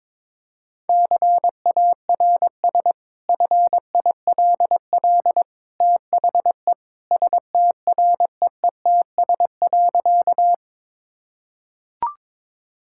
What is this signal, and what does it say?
Morse code, heard as a keyed tone, about 21 wpm, 700 Hz. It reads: CARS FILL THE STREETS.
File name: practice / 22wpm / Easy Sentences with Hints - 22wpm